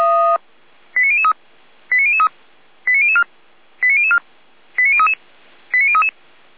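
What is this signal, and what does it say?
Five-tone selective call (ZVEI), heard as quick runs of five tones, tones of about 70 ms. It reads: 89003 89003 89004 89004 89033 89033
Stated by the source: Five-tone ZVEI selective call 89010 received by SDR, logged by fire brigade Feuerwehr Orlamünde.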